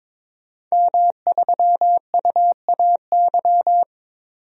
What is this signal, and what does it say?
Morse code, heard as a keyed tone, about 22 wpm, 700 Hz. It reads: M3UAY
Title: DXCC-100 Call Signs - 5 Characters - 22wpm